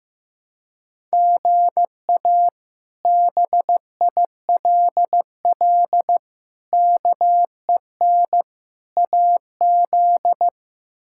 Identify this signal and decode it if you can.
Morse code, heard as a keyed tone, about 15 wpm, 700 Hz. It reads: GA BILL KEN AZ
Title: K1USN SST QSO Practice - 15wpm